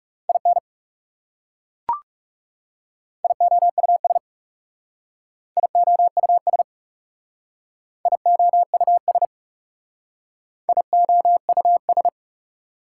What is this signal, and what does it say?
Morse code, heard as a keyed tone, about 37 wpm, 700 Hz. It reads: IN IOUS IOUS IOUS IOUS
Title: Speed-Racing Prefixes and Suffixes 30wpm